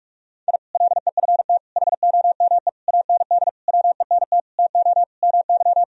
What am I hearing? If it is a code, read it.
Morse code, heard as a keyed tone, about 46 wpm, 700 Hz. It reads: I LEFT HOME AND WENT TO MY
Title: Sentences from Top 1000 Words - Review Part 3 - 45wpm